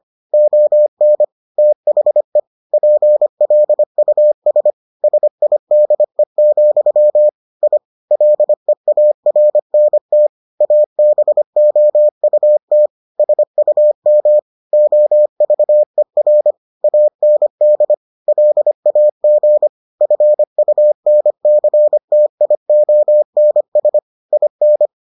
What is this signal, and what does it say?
Morse code, heard as a keyed tone, about 25 wpm, 600 Hz. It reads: ON THE PLUS SIDE, I LEARNT ABOUT SUM OVER AND LAG FUNCTIONS IN